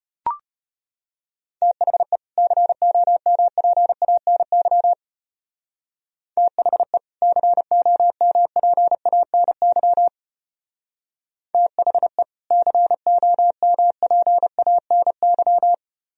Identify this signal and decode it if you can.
Morse code, heard as a keyed tone, about 31 wpm, 700 Hz. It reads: THE COMPANY THE COMPANY THE COMPANY
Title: Speed-Racing - Sets of 2 Words 25wpm